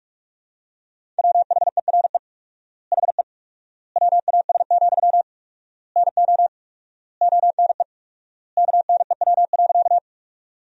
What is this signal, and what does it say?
Morse code, heard as a keyed tone, about 42 wpm, 700 Hz. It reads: WHERE HE WAS, NO ONE KNEW.
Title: 4x Word-Spacing - Sentences from Top 400 Words - 45wpm